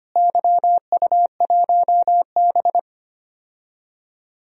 Morse code, heard as a keyed tone, about 25 wpm, 700 Hz. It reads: YU1B
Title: DXCC-100 Call Signs - 4 Characters - Repeated 3x - 25wpm